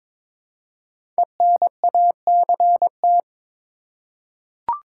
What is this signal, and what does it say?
Morse code, heard as a keyed tone, about 23 wpm, 700 Hz. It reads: ENACT
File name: 5 Letter Words - First 3 Letters Given - 22wpm